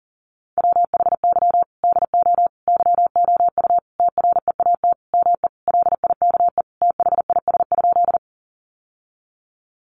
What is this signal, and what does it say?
Morse code, heard as a keyed tone, about 40 wpm, 700 Hz. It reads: WHY DO YOU TREAT ME LIKE THIS?
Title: Sentences from Top 1300 Words 40wpm